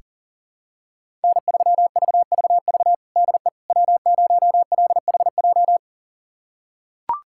Morse code, heard as a keyed tone, about 40 wpm, 700 Hz. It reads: N3VVV DE W0LHJ
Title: USA Callsign DE Callsign - 5 Characters - 40wpm